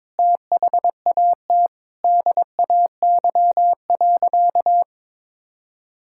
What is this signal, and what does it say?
Morse code, heard as a keyed tone, about 22 wpm, 700 Hz. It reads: THAT DAY.
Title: Sentences from Top 1000 Words - Review Part 5 - 22wpm